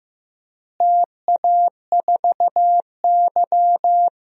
Morse code, heard as a keyed tone, about 15 wpm, 700 Hz. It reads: TA4Y
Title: DXCC-100 Call Signs - 4 Characters - Repeated 3x - 15wpm